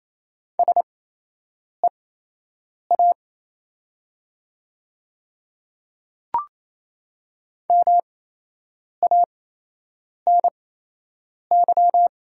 Morse code, heard as a keyed tone, about 27 wpm, spaced as for 8 wpm, 700 Hz. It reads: SEA MANY
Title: 8x Character Spacing - Top 300 Words - 28wpm